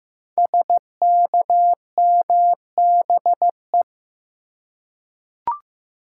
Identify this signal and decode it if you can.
Morse code, heard as a keyed tone, about 15 wpm, 700 Hz. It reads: SKMBE
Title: Code Groups - 5 Letters - 15wpm